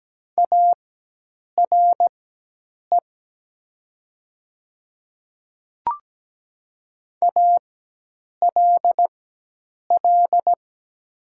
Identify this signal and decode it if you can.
Morse code, heard as a keyed tone, about 17 wpm, spaced as for 8 wpm, 700 Hz. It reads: ARE ALL
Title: 4x Character Spacing - Top 100 Words - 17wpm